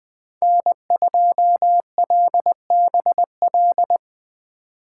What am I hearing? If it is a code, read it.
Morse code, heard as a keyed tone, about 21 wpm, 700 Hz. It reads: N2LBL
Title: USA Call Signs - 5 Characters - 3x Character Spacing - 20wpm